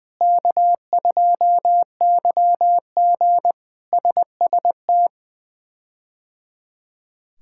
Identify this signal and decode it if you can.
Morse code, heard as a keyed tone, about 20 wpm, 700 Hz. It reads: K2YG SST
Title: K1USN SST QSO Practice - 20wpm